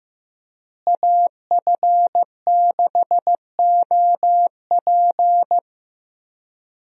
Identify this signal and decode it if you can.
Morse code, heard as a keyed tone, about 15 wpm, 700 Hz. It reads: AF6OP
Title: USA Call Signs - 5 Characters - Repeated 2x - 15wpm